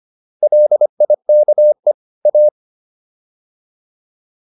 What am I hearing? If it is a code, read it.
Morse code, heard as a keyed tone, about 25 wpm, 600 Hz. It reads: LIKE A